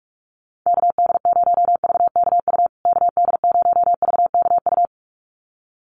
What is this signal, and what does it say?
Morse code, heard as a keyed tone, about 44 wpm, 700 Hz. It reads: KD0VKU KD0VKU
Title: USA Call Signs Repeated 2x 45wpm